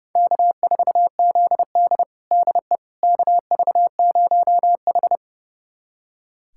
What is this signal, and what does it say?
Morse code, heard as a keyed tone, about 30 wpm, 700 Hz. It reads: K4ZD DE KV0H
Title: USA Callsign DE Callsign - 4 Characters - 30wpm